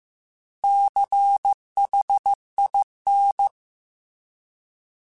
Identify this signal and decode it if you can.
Morse code, heard as a keyed tone, about 15 wpm, 800 Hz. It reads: CHIN